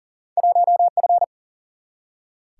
Morse code, heard as a keyed tone, about 40 wpm, 700 Hz. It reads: 1F
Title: Single-Letter Single-Number 40wpm